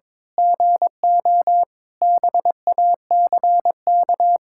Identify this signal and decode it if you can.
Morse code, heard as a keyed tone, about 22 wpm, 700 Hz. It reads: GO BACK
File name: Sentences with 7-Word Phrases - 22wpm